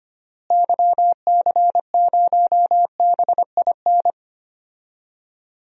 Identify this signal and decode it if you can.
Morse code, heard as a keyed tone, about 25 wpm, 700 Hz. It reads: YC0BIN
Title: DXCC-100 Call Signs - 6 Characters - Repeated 3x - 25wpm